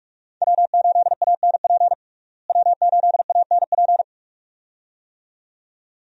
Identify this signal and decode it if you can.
Morse code, heard as a keyed tone, about 46 wpm, 700 Hz. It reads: W8ANP W8ANP
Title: USA Call Signs - 5 Characters - Repeated 2x - 45wpm